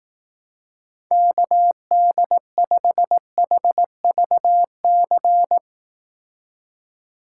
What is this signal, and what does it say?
Morse code, heard as a keyed tone, about 18 wpm, 700 Hz. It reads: KD5HVC
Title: Speed Racing - USA 2x3 Call Signs 15wpm